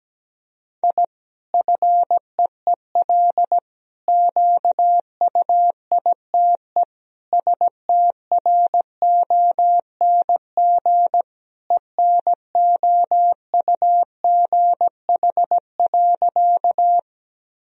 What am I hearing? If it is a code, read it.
Morse code, heard as a keyed tone, about 17 wpm, 700 Hz. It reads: I FEEL QUITE STRONG ENOUGH.
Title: Sentences from Top 1000 Words - Review Part 3 - 17wpm